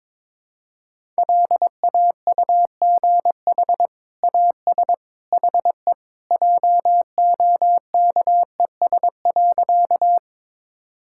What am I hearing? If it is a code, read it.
Morse code, heard as a keyed tone, about 22 wpm, 700 Hz. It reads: LAUGH AS HE JOKES.
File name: practice / 22wpm / Sentences with 2-Word Phrases - 22wpm